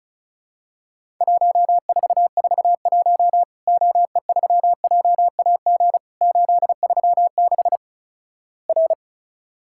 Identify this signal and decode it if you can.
Morse code, heard as a keyed tone, about 36 wpm, 700 Hz. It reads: 1441 OE3JAG 836 R
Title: Worked All Europe DX Contest - QTC Exchange - 35wpm